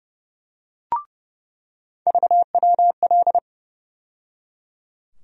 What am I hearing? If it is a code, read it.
Morse code, heard as a keyed tone, about 30 wpm, 700 Hz. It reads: VWL
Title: USA Call Sign Postfixes - 3 Letters 30wpm